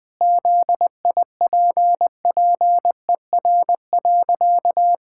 Morse code, heard as a keyed tone, about 21 wpm, 700 Hz. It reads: ZIPPER.
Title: Pangram Challenge v2 - 20wpm